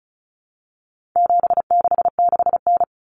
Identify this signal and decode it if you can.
Morse code, heard as a keyed tone, about 34 wpm, 700 Hz. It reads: 766N
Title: Common Cut Numbers - 4-Digit 35wpm